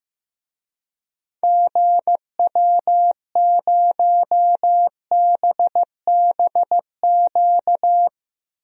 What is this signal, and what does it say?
Morse code, heard as a keyed tone, about 15 wpm, 700 Hz. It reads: GW0BBQ
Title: DXCC-100 Call Signs - 6 Characters - 15wpm